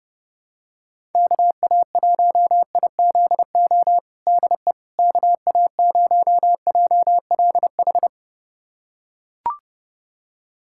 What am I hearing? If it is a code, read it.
Morse code, heard as a keyed tone, about 30 wpm, 700 Hz. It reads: KA1IZO DE KA0JLH
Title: USA Callsign DE Callsign - 6 Characters - 30wpm